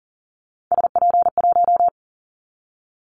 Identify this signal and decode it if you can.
Morse code, heard as a keyed tone, about 40 wpm, 700 Hz. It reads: SP1
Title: Callsign Prefixes for Top 25 DXCC - Repeated 3x - 40wpm